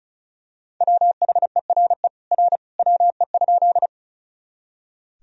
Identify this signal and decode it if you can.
Morse code, heard as a keyed tone, about 36 wpm, 700 Hz. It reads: WHERE R WE?
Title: Sentences from Top 200 Words and Abbreviations 35wpm